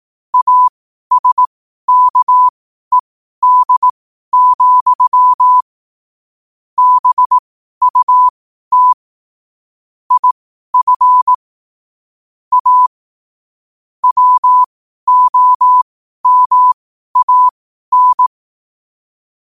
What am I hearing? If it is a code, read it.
Morse code, heard as a keyed tone, about 18 wpm, spaced as for 12 wpm, 1000 Hz. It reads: ASKED, BUT IF A WOMAN
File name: qotdfiles / 12wpm